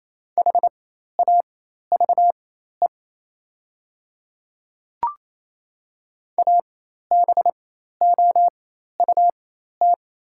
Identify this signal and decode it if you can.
Morse code, heard as a keyed tone, about 28 wpm, spaced as for 13 wpm, 700 Hz. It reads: HAVE ABOUT
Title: 4x Character Spacing - Top 100 Words - 28wpm